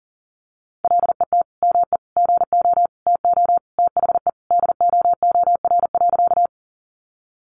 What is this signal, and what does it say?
Morse code, heard as a keyed tone, about 40 wpm, 700 Hz. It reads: LET ME GO TO THE DOOR.
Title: Sentences from Top 300 Words 40wpm